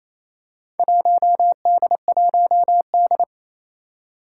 Morse code, heard as a keyed tone, about 28 wpm, 700 Hz. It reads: 1D1D
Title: Four Challenging Letter-Number Combinations 28wpm